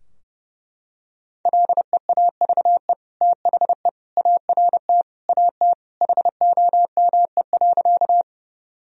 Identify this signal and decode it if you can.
Morse code, heard as a keyed tone, about 32 wpm, 700 Hz. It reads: LEAVE THE ART AT HOME.
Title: Sentences from Top 1000 Words - Review Part 3 - 30wpm